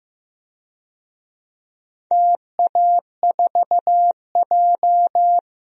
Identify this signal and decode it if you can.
Morse code, heard as a keyed tone, about 15 wpm, 700 Hz. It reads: TA4J
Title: DXCC-100 Call Signs - 4 Characters - 15wpm